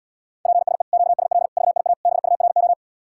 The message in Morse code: PSE QSL VIA BURO

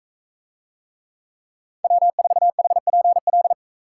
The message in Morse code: W4HPL

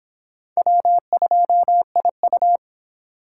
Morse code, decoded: W2IU

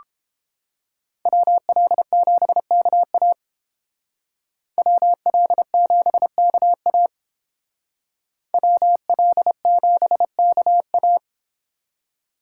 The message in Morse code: WL7KA WL7KA WL7KA

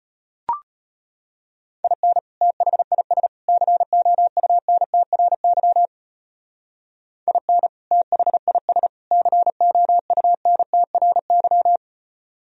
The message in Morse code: IN THIS COUNTRY IN THIS COUNTRY

